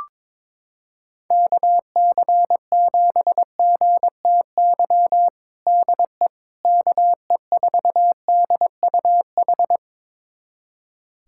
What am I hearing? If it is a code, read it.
Morse code, heard as a keyed tone, about 21 wpm, 700 Hz. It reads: KC7GTY DE KE4DUH
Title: USA Callsign DE Callsign - 6 Characters - 22wpm